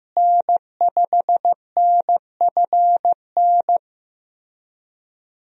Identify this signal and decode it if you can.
Morse code, heard as a keyed tone, about 16 wpm, 700 Hz. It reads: N5NFN
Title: USA Call Signs - 5 Characters - Repeated 3x - 15wpm